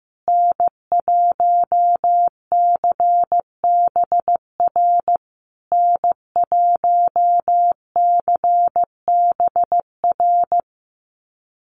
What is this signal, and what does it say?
Morse code, heard as a keyed tone, about 14 wpm, 700 Hz. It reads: N1CBR N1CBR
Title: USA Call Signs Repeated 3x 15wpm